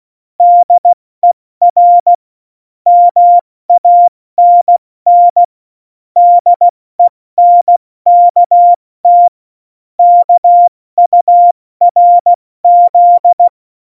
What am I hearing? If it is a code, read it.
Morse code, heard as a keyed tone, about 16 wpm, 700 Hz. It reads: DER MANN DENKT KURZ
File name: Morsekurs / Witze / Witz 15. Oktober 2019, 70 F80